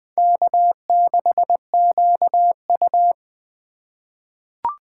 Morse code, heard as a keyed tone, about 20 wpm, 700 Hz. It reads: K6QU
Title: USA Call Signs - 4 Characters - 3x Character Spacing - 20wpm